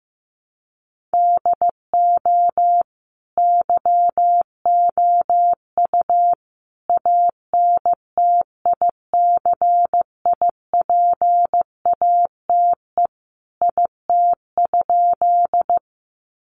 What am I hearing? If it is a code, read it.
Morse code, heard as a keyed tone, about 15 wpm, 700 Hz. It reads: DO YOU ANTICIPATE IT?
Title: Sentences from Top 1500 Words 15wpm